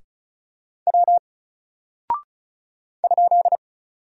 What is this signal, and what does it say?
Morse code, heard as a keyed tone, about 34 wpm, 700 Hz. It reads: W ?
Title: Single Character (Letter:Number:Punctuation:Prosign) 35wpm